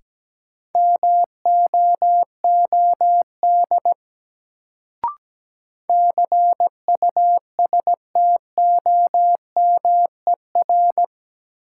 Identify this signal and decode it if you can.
Morse code, heard as a keyed tone, about 17 wpm, 700 Hz. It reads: MOOD CUSTOMER
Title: Sentences from Top 1100 Words - New Word Drill Alt - 17wpm